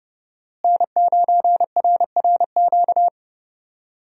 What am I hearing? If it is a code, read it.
Morse code, heard as a keyed tone, about 30 wpm, 700 Hz. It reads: N9RRQ
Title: USA Call Signs - 5 Characters - Repeated 2x - 30wpm